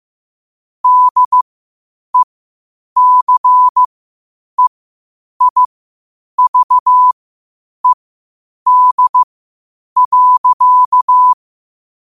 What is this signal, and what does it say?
Morse code, heard as a keyed tone, about 15 wpm, spaced as for 8 wpm, 1000 Hz. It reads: DECEIVED.
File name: qotdfiles / 8wpm